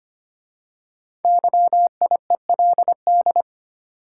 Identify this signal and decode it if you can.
Morse code, heard as a keyed tone, about 25 wpm, 700 Hz. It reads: YIELD
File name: 5 Letter Words - First 3 Letters Given - 25wpm